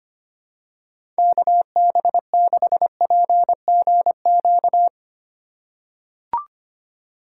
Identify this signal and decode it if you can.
Morse code, heard as a keyed tone, about 25 wpm, 700 Hz. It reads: KB6PGQ
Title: USA Call Signs - 4x Character Spacing - 25wpm